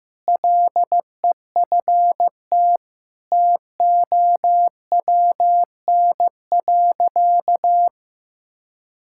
Morse code, heard as a keyed tone, about 15 wpm, 700 Hz. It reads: LEFT TOWN.
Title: Sentences from Top 1000 Words - Review Part 6 - 15wpm